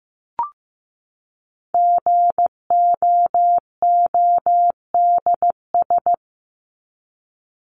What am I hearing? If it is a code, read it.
Morse code, heard as a keyed tone, about 15 wpm, 700 Hz. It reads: GOODS
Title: ES:S Suffix - Top 100 Words with Base Words 15wpm